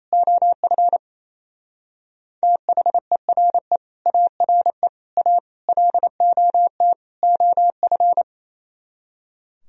Morse code, heard as a keyed tone, about 36 wpm, 700 Hz. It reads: OF THERE ARE A LOT OF